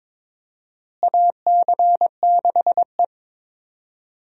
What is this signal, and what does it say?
Morse code, heard as a keyed tone, about 22 wpm, 700 Hz. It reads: AC6E
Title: USA Call Signs - 4 Characters - Repeated 2x - 22wpm